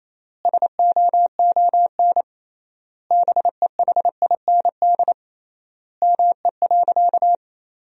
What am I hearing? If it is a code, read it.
Morse code, heard as a keyed tone, about 28 wpm, 700 Hz. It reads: SOON BEHIND ME.
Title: 3x Word-Spacing - Sentences from Top 500 Words - 28wpm